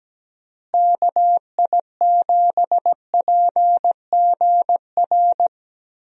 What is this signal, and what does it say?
Morse code, heard as a keyed tone, about 17 wpm, 700 Hz. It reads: KI7PGR
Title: USA Call Signs - 6 Characters - Repeated 3x - 17wpm